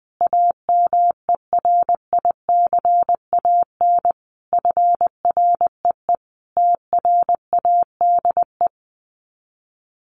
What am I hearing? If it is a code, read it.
Morse code, heard as a keyed tone, about 20 wpm, 700 Hz. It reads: AMERICAN FREE TRADE
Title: Sets of 5 Words Encore 20wpm